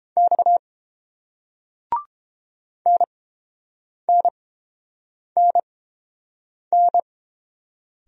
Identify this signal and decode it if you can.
Morse code, heard as a keyed tone, about 30 wpm, spaced as for 7 wpm, 700 Hz. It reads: X NNNN